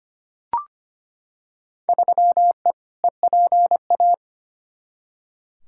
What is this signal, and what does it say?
Morse code, heard as a keyed tone, about 25 wpm, 700 Hz. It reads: 3E EPA
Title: Field Day Exchanges Encore - 25wpm